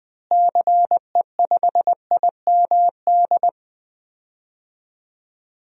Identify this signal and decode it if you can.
Morse code, heard as a keyed tone, about 20 wpm, 700 Hz. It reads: CE5IMD
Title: DXCC-100 Call Signs - 6 Characters - Repeated 2x - 20wpm